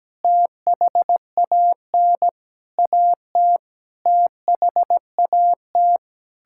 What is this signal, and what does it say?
Morse code, heard as a keyed tone, about 17 wpm, 700 Hz. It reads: THAN AT THAT
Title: Sentences from Top 1000 Words - Review Part 6 - 17wpm